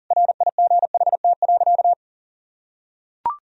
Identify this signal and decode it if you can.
Morse code, heard as a keyed tone, about 40 wpm, 700 Hz. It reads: RIGHT.
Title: Sentences from Top 1000 Words - Review Part 1 - 40wpm